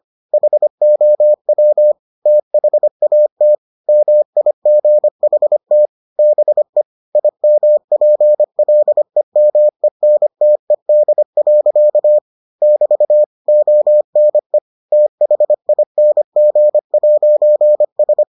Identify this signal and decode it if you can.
Morse code, heard as a keyed tone, about 25 wpm, 600 Hz. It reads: HOW THAT MIGHT BE IMPLEMENTED. <BT> ONE THING'S